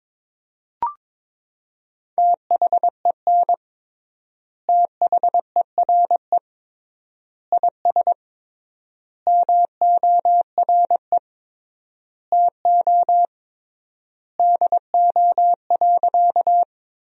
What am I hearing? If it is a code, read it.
Morse code, heard as a keyed tone, about 22 wpm, 700 Hz. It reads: THEN THERE IS MORE TO DO.
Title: 3x Word-Spacing - Sentences from Top 100 Words - 22wpm